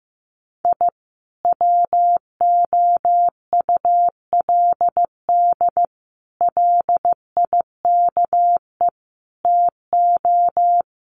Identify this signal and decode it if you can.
Morse code, heard as a keyed tone, about 15 wpm, 700 Hz. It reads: I WOULD LIKE TO
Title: Sets of 5 Words Encore 15wpm